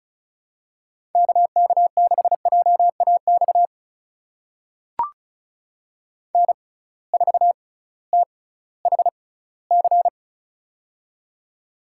Morse code, heard as a keyed tone, about 36 wpm, spaced as for 19 wpm, 700 Hz. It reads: KK6JAX N4THC
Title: USA Call Signs - 6x Character Spacing - 35wpm